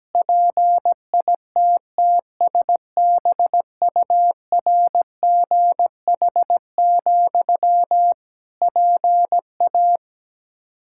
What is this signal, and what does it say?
Morse code, heard as a keyed tone, about 17 wpm, 700 Hz. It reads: PITTSBURGH, PA
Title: Practice QSO - Exchange 1 - Beginning - 17wpm